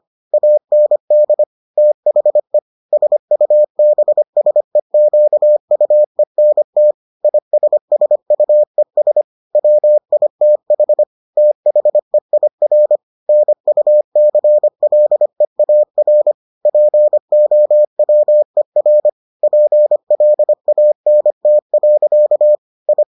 Morse code, heard as a keyed tone, about 25 wpm, 600 Hz. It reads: AND THE SUBSEQUENT ISSUES WITH THEIR NUCLEAR POWER PLANT. I